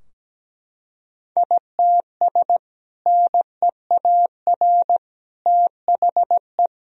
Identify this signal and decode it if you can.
Morse code, heard as a keyed tone, about 17 wpm, 700 Hz. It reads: ITS NEAR THE
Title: Sentences from Top 1000 Words - Review Part 2 - 17wpm